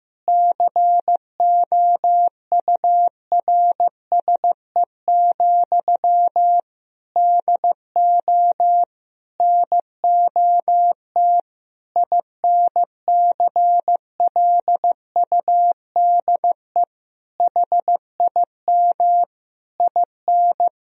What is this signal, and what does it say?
Morse code, heard as a keyed tone, about 16 wpm, 700 Hz. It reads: COURSE, DO NOT INCLUDE HIM IN
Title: Sentences from Top 1000 Words - Review Part 3 - 15wpm